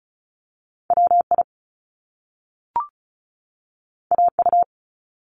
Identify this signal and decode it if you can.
Morse code, heard as a keyed tone, about 36 wpm, 700 Hz. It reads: WI AU
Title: Top Two Letter Combinations in Words 35wpm